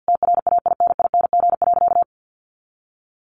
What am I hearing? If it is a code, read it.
Morse code, heard as a keyed tone, about 50 wpm, 700 Hz. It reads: TRAINING.
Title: Sentences from Top 1400 Words 50wpm